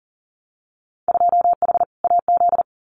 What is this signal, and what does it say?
Morse code, heard as a keyed tone, about 42 wpm, 700 Hz. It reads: 2H AZ
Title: Winter Field Day Exchanges 40wpm